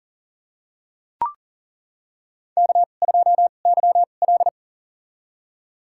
Morse code, heard as a keyed tone, about 40 wpm, spaced as for 29 wpm, 700 Hz. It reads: K2YL
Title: USA Call Signs - 2x Character Spacing - 40wpm